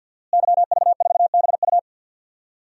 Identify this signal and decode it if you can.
Morse code, heard as a keyed tone, about 51 wpm, 700 Hz. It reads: YV4BU
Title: DXCC-100 Call Signs - 5 Characters - Repeated 3x - 50wpm